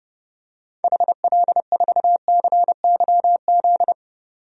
Morse code, heard as a keyed tone, about 30 wpm, 700 Hz. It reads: HL4CYZ
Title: DXCC-100 Call Signs - 6 Characters - 30wpm